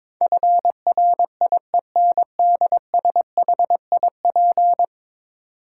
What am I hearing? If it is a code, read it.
Morse code, heard as a keyed tone, about 22 wpm, 700 Hz. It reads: FRIENDSHIP
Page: morse-code-ninja-practice-22wpm